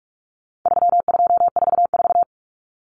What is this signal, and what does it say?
Morse code, heard as a keyed tone, about 44 wpm, 700 Hz. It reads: 3244
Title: Common Cut Numbers - 4-Digit 45wpm